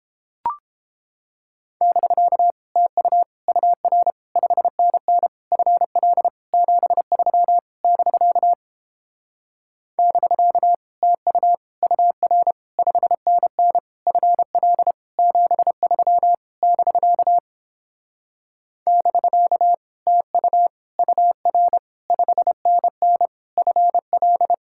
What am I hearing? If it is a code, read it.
Morse code, heard as a keyed tone, about 31 wpm, 700 Hz. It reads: <BK> TU UR 5NN FL 73 <BK> <BK> TU UR 5NN FL 73 <BK> <BK> TU UR 5NN FL